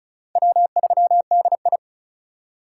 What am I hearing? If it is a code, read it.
Morse code, heard as a keyed tone, about 34 wpm, 700 Hz. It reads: W3DI
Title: USA Call Signs - 4 Characters - 4x Character Spacing - 35wpm